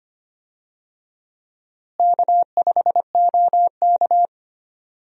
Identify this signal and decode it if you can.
Morse code, heard as a keyed tone, about 25 wpm, 700 Hz. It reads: K5OK
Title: USA Call Signs - 4 Characters - Repeated 2x - 25wpm